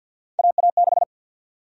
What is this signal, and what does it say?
Morse code, heard as a keyed tone, about 49 wpm, 700 Hz. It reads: AA6
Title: USA Call Sign Prefixes - 50wpm